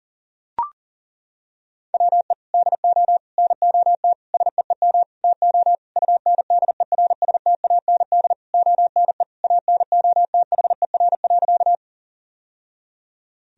Morse code, heard as a keyed tone, about 40 wpm, 700 Hz. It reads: WE DO NOT SEEM TO UNDERSTAND ONE ANOTHER.
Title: Sentences from Top 1000 Words - Review Part 1 - 40wpm